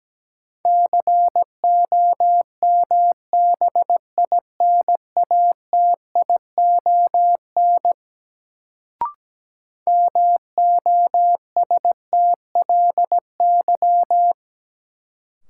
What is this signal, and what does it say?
Morse code, heard as a keyed tone, about 17 wpm, 700 Hz. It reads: COMBINATION MOSTLY
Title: Sentences from Top 1300 Words - New Word Drill Alt - 17wpm